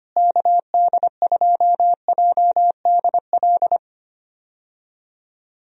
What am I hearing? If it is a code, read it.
Morse code, heard as a keyed tone, about 25 wpm, 700 Hz. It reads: KD2JDL
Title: USA Call Signs - 6 Characters - Repeated 3x - 25wpm